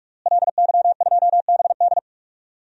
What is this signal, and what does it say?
Morse code, heard as a keyed tone, about 45 wpm, 700 Hz. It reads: LY2BD
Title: DXCC-100 Call Signs - 5 Characters - Repeated 3x - 45wpm